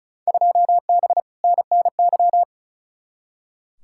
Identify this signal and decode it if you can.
Morse code, heard as a keyed tone, about 36 wpm, 700 Hz. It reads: 2B NNY
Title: Field Day Exchanges Encore - 35wpm